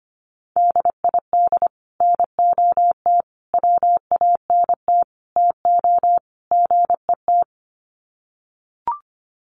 Morse code, heard as a keyed tone, about 25 wpm, 700 Hz. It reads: DID NOT WANT TO GET
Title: Sets of 5 Words Encore2 25wpm